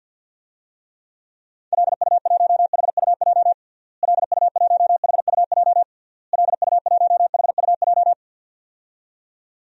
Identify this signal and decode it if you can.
Morse code, heard as a keyed tone, about 48 wpm, 700 Hz. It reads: LU1HUJ LU1HUJ LU1HUJ
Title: DXCC-100 Call Signs - 6 Characters - Repeated 3x - 50wpm